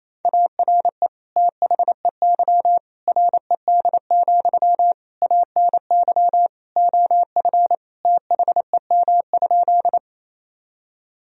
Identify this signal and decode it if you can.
Morse code, heard as a keyed tone, about 28 wpm, 700 Hz. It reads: ARE THEY RED, ANY OF THEM?
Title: Sentences from Top 1000 Words - Review Part 2 - 28wpm